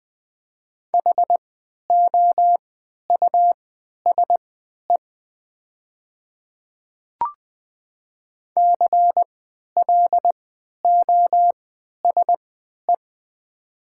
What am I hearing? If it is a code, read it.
Morse code, heard as a keyed tone, about 20 wpm, spaced as for 11 wpm, 700 Hz. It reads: HOUSE CLOSE
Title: 3x Character Spacing - Top 300 Words - 20wpm